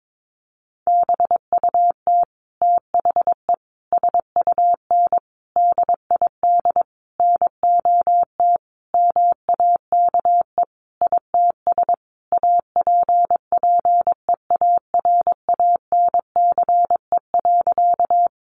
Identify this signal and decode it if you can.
Morse code, heard as a keyed tone, about 22 wpm, 700 Hz. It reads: BUT THE SUN DID NOT MAKE ITS APPEARANCE.
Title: Sentences from Top 1100 Words 22wpm